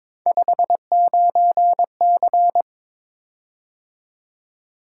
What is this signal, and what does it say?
Morse code, heard as a keyed tone, about 22 wpm, 700 Hz. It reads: 59C
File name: practice / 22wpm / Three Letter-Number Combinations - 22wpm